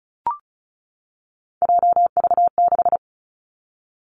Morse code, heard as a keyed tone, about 36 wpm, 700 Hz. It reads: JV6
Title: Three Challenging Letter-Number Combinations 35wpm